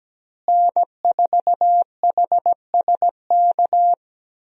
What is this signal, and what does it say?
Morse code, heard as a keyed tone, about 17 wpm, 700 Hz. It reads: N4HSK
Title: USA Call Signs - 5 Characters - Repeated 3x - 17wpm